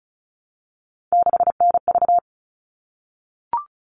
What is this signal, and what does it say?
Morse code, heard as a keyed tone, about 34 wpm, 700 Hz. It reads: 6NV
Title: All Cut Numbers - 3-Digit 35wpm